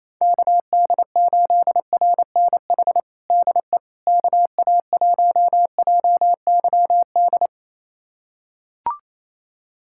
Morse code, heard as a keyed tone, about 28 wpm, 700 Hz. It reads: KD8RNH DE KA1JYD
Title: USA Callsign DE Callsign 28wpm